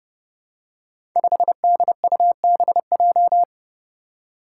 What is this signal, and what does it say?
Morse code, heard as a keyed tone, about 30 wpm, 700 Hz. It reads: 5DUBJ
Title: Five Challenging Letter-Number Combinations 30wpm